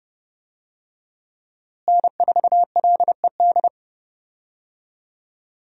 Morse code, heard as a keyed tone, about 31 wpm, 700 Hz. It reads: N4LED
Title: USA Call Signs - 5 Characters - Repeated 2x - 30wpm